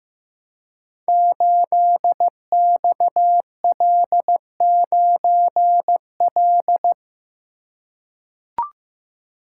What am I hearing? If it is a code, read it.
Morse code, heard as a keyed tone, about 15 wpm, 700 Hz. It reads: 8XL9L
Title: Code Groups - 5 Letter:Numbers - 15wpm